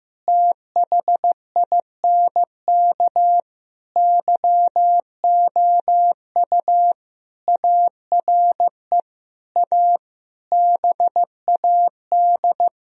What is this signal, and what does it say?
Morse code, heard as a keyed tone, about 15 wpm, 700 Hz. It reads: THINK YOU ARE A BAD
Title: Sentences from Top 1000 Words - Review Part 3 - 15wpm